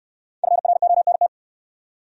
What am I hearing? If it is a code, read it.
Morse code, heard as a keyed tone, about 64 wpm, 700 Hz. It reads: FRONT